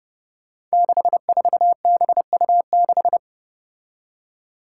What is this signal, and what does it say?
Morse code, heard as a keyed tone, about 32 wpm, 700 Hz. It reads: 64BU6
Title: Five Challenging Letter-Number Combinations 30wpm